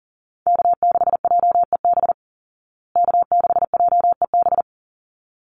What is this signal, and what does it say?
Morse code, heard as a keyed tone, about 40 wpm, 700 Hz. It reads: K6JEB K6JEB
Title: Call Signs Repeated 3x - 40wpm